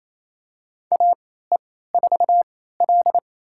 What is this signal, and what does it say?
Morse code, heard as a keyed tone, about 28 wpm, spaced as for 16 wpm, 700 Hz. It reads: AE4L
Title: USA Call Signs - 4 Characters - 3x Character Spacing - 28wpm